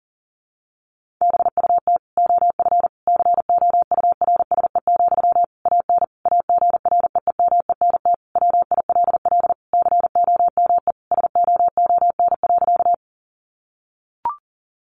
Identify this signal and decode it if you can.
Morse code, heard as a keyed tone, about 40 wpm, 700 Hz. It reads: BUT OF COURSE, AN AGREEMENT WILL COME SOON.
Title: Sentences from Top 1500 Words 40wpm